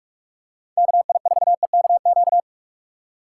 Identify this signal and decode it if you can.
Morse code, heard as a keyed tone, about 44 wpm, 700 Hz. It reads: KI4EKQ
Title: USA Call Signs - 6 Characters - 3x Character Spacing - 45wpm